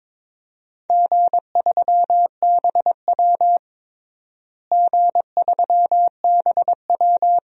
Morse code, heard as a keyed tone, about 21 wpm, 700 Hz. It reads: G3BW G3BW
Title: DXCC-100 Call Signs - 4 Characters - Repeated 3x - 22wpm